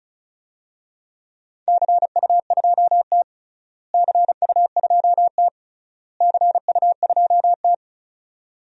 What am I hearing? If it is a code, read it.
Morse code, heard as a keyed tone, about 36 wpm, 700 Hz. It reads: CU2T CU2T CU2T
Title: DXCC-100 Call Signs - 4 Characters - Repeated 3x - 35wpm